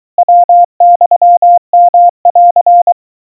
Morse code (Japanese Wabun code), heard as a keyed tone, about 23 wpm, 700 Hz. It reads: ヤユヨン
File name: kawa / Morse sample / aiueo 80 cpm 700 Hz Crispy